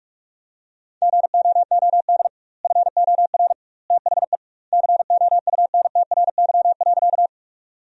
Morse code, heard as a keyed tone, about 43 wpm, 700 Hz. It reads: GOOD FOR THE COUNTRY.
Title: 2x Word-Spacing - Sentences from Top 300 Words - 45wpm